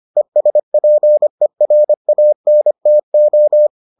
Morse code, heard as a keyed tone, about 25 wpm, 600 Hz. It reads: ESPERANTO